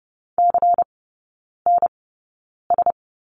Morse code, heard as a keyed tone, about 30 wpm, spaced as for 9 wpm, 700 Hz. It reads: CNS
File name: Lesson 16e - Letter c - Three Characters 30wpm